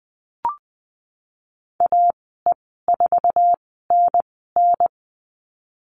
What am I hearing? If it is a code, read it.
Morse code, heard as a keyed tone, about 20 wpm, spaced as for 15 wpm, 700 Hz. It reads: AE4NN
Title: 088 Lesson 15g - Number 5 - Call Signs 20wpm